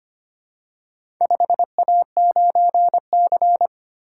Morse code, heard as a keyed tone, about 25 wpm, 700 Hz. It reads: 5A9C
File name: practice / 25wpm / Four Letter-Number Combinations - 25wpm